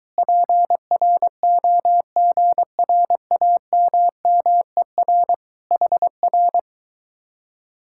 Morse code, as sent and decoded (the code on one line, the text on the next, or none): .--. .-. --- --. .-. .- -- -- . .-. / .... .-.
PROGRAMMER HR